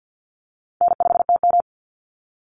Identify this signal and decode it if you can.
Morse code, heard as a keyed tone, about 50 wpm, 700 Hz. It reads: N5TM
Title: Call Signs - 50wpm